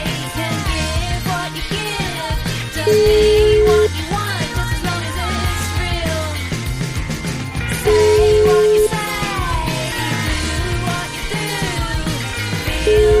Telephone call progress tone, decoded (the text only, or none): ringback tone